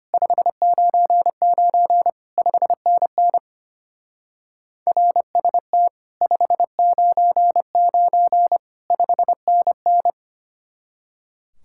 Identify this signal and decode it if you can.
Morse code, heard as a keyed tone, about 31 wpm, 700 Hz. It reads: 599 5NN RST 599 5NN